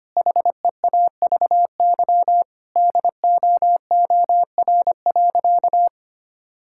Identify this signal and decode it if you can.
Morse code, heard as a keyed tone, about 25 wpm, 700 Hz. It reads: HEAVY DOOR.
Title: Sentences from Top 1600 Words 25wpm